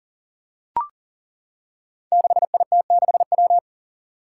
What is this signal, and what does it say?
Morse code, heard as a keyed tone, about 38 wpm, 700 Hz. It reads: 6IT6W
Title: Five Challenging Letter-Number Combinations 40wpm